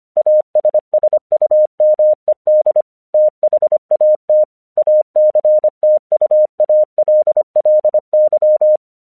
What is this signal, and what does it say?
Morse code, heard as a keyed tone, about 25 wpm, 610 Hz. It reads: ASSUMED THAT ACTUALLY
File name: 1 English 25wpm / AQC32